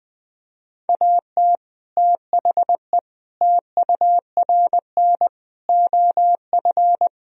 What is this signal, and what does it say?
Morse code, heard as a keyed tone, about 20 wpm, 700 Hz. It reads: AT THE TURN OF
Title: Speed-Racing - Sets of 5 Words 17wpm